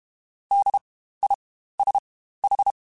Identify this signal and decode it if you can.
Morse code, heard as a keyed tone, about 32 wpm, spaced as for 15 wpm, 800 Hz. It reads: DISH